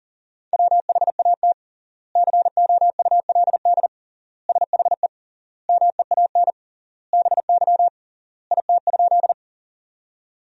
Morse code, heard as a keyed tone, about 40 wpm, 700 Hz. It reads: WHAT COULD SHE MEAN BY IT?
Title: 3x Word-Spacing - Sentences from Top 200 Words - 40wpm